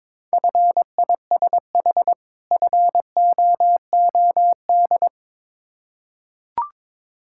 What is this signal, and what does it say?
Morse code, heard as a keyed tone, about 22 wpm, 690 Hz. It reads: FISH FOOD
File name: Difficult Words with Hints - 22wpm